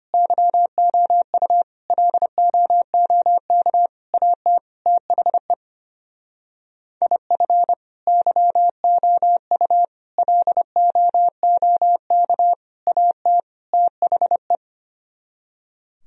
Morse code, heard as a keyed tone, about 31 wpm, 700 Hz. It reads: YOU LOOK AT THE IF YOU LOOK AT THE
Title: Speed-Racing - Sets of 5 Words 25wpm